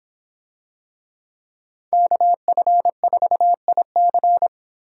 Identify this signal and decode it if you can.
Morse code, heard as a keyed tone, about 26 wpm, 700 Hz. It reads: KF4IC